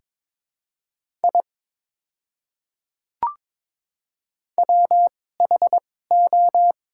Morse code, heard as a keyed tone, about 22 wpm, spaced as for 16 wpm, 690 Hz. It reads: I WHO
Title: 2x Character Spacing - Top 200 Words - 22wpm